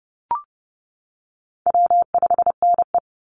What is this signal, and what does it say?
Morse code, heard as a keyed tone, about 30 wpm, 700 Hz. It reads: W5NE
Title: Call Signs - 30wpm